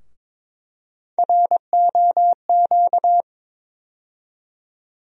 Morse code, heard as a keyed tone, about 22 wpm, 700 Hz. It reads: ROQ
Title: USA Call Sign Postfixes - 3 Letters 22wpm